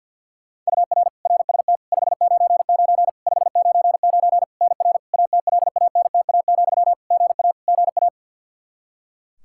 Morse code, heard as a keyed tone, about 50 wpm, 700 Hz. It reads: UR RST 599 599 NR ATLANTA, GA GA
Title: Practice QSO - Exchange 1 - Beginning - 50wpm